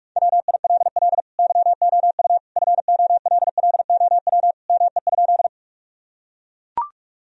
Morse code, heard as a keyed tone, about 42 wpm, 700 Hz. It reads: WILL YOU FOLLOW ME?
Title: Sentences from Top 1000 Words - Review Part 1 - 45wpm